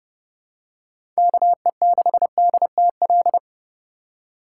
Morse code, heard as a keyed tone, about 30 wpm, 700 Hz. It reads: KE6DTL
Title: USA Call Signs - 6 Characters - 4x Character Spacing - 30wpm